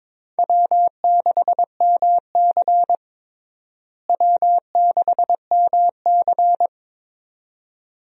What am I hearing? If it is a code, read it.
Morse code, heard as a keyed tone, about 22 wpm, 700 Hz. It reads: W6MC W6MC
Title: USA Call Signs - 4 Characters - Repeated 2x - 22wpm